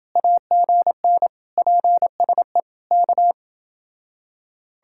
Morse code, heard as a keyed tone, about 29 wpm, 700 Hz. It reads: AGN PSE K